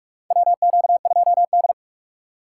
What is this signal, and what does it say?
Morse code, heard as a keyed tone, about 46 wpm, 700 Hz. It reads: WQ2D
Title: USA Call Signs - 4 Characters - 4x Character Spacing - 45wpm